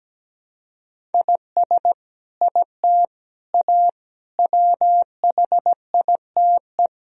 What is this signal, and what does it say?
Morse code, heard as a keyed tone, about 17 wpm, 700 Hz. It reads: IS IT A WHITE
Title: Sentences from Top 1000 Words - Review Part 1 - 17wpm